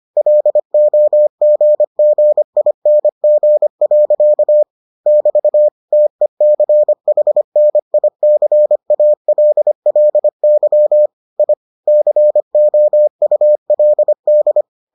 Morse code, heard as a keyed tone, about 25 wpm, 600 Hz. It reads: LOGGING. <BT> TECHNICALLY I COULD